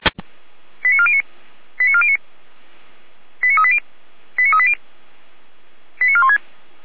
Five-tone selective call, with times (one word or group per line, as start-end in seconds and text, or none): 0.8-1.2 89409
1.8-2.2 89409
3.4-3.8 89380
4.4-4.7 89380
6.0-6.4 89416